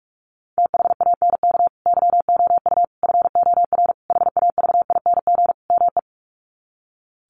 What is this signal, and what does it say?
Morse code, heard as a keyed tone, about 44 wpm, 700 Hz. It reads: THANK YOU FOR HAVING ME
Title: Sets of 5 Words Encore 45wpm